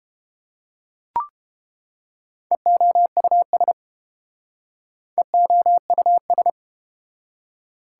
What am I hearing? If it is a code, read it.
Morse code, heard as a keyed tone, about 29 wpm, 700 Hz. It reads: EOUS EOUS